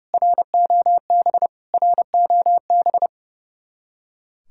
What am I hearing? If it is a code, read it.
Morse code, heard as a keyed tone, about 35 wpm, 700 Hz. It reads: ROB ROB
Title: Speed-Racing Common QSO Phrases - 30wpm